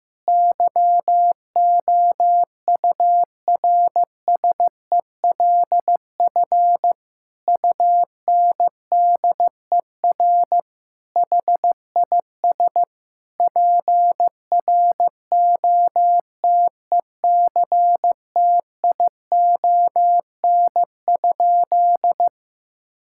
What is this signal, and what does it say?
Morse code, heard as a keyed tone, about 15 wpm, 700 Hz. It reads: YOURSELF UNDER HIS PROTECTION?
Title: Sentences from Top 1600 Words 15wpm